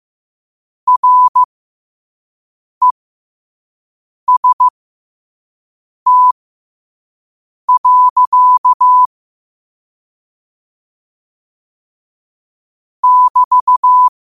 Morse code, heard as a keyed tone, about 15 wpm, spaced as for 5 wpm, 1000 Hz. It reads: REST. <BT>
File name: qotdfiles / 5wpm